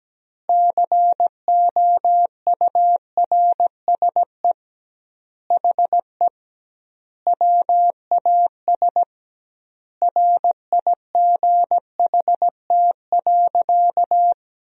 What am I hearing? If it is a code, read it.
Morse code, heard as a keyed tone, about 17 wpm, 700 Hz. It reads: COURSE HE WAS RIGHT.
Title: 2x Word-Spacing - Sentences from Top 500 Words - 17wpm